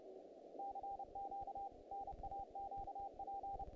Morse code, heard as a keyed tone, about 30 wpm, 790 Hz, with noise. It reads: CQ NAQP
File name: contests / naqp-cw-k5tr-op-k5pi- -1800